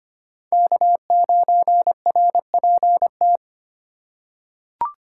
Morse code, heard as a keyed tone, about 25 wpm, 700 Hz. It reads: K9RPT
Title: USA Call Signs - 5 Characters - 4x Character Spacing - 25wpm